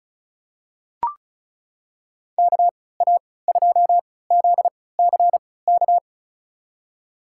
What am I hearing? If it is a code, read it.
Morse code, heard as a keyed tone, about 36 wpm, spaced as for 20 wpm, 700 Hz. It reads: KA2ZCK
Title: USA Call Signs - 6 Characters - 3x Character Spacing - 35wpm